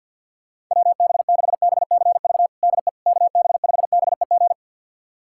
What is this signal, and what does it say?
Morse code, heard as a keyed tone, about 52 wpm, 700 Hz. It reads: WB6BCV DE KB5BEG
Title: USA Callsign DE Callsign - 6 Characters - 50wpm